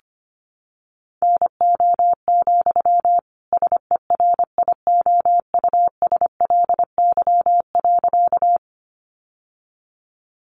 Morse code, heard as a keyed tone, about 25 wpm, 700 Hz. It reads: NO, SERIOUSLY.
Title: Sentences from Top 1200 Words 25wpm